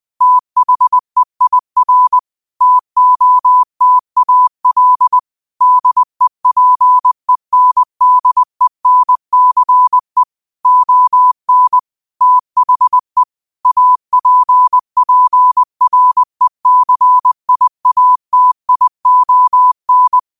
Morse code, heard as a keyed tone, about 20 wpm, 1000 Hz. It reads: THEIR TOTAL DEPENDENCE ON THE APPRECIATION